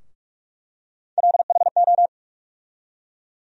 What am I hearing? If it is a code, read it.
Morse code, heard as a keyed tone, about 48 wpm, 700 Hz. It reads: LHO
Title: USA Call Sign Postfixes - 3 Letters 45wpm